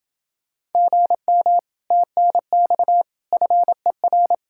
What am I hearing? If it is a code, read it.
Morse code, heard as a keyed tone, about 30 wpm, 700 Hz. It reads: GM TNX FER